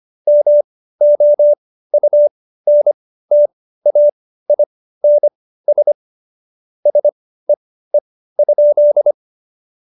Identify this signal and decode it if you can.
Morse code, heard as a keyed tone, about 25 wpm, spaced as for 15 wpm, 590 Hz. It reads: MOUNTAINS SEE?
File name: Jokes-25-char-15-eff